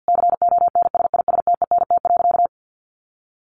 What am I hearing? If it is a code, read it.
Morse code, heard as a keyed tone, about 50 wpm, 700 Hz. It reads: CONSISTENT.
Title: Sentences from Top 1500 Words 50wpm